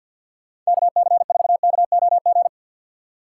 Morse code, heard as a keyed tone, about 61 wpm, 700 Hz. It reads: KC4KOG
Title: Speed Racing - USA 2x3 Call Signs 50wpm